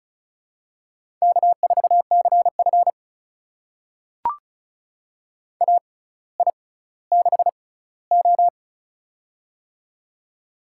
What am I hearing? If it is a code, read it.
Morse code, heard as a keyed tone, about 36 wpm, spaced as for 18 wpm, 700 Hz. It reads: K4CF AI6O
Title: USA Call Signs - 6x Character Spacing - 35wpm